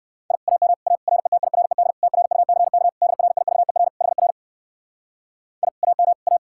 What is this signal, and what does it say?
Morse code, heard as a keyed tone, about 59 wpm, 700 Hz. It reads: I AM A RETIRED TRUCK DRIVER HR I AM A